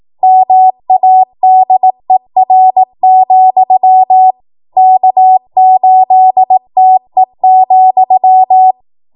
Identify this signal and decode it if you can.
Morse code, heard as a keyed tone, about 18 wpm, 750 Hz. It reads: MADER, K8TE,
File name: Morse / 18WPM